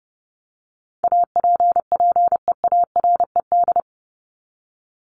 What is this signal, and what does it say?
Morse code, heard as a keyed tone, about 30 wpm, 700 Hz. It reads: APPEARED
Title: ED Suffix - Top 100 Words with Base Words 30wpm